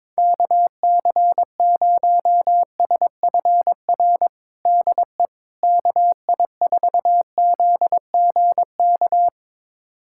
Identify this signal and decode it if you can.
Morse code, heard as a keyed tone, about 22 wpm, 700 Hz. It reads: KC0SFR DE KI4ZGK
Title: USA Callsign DE Callsign - 6 Characters - 22wpm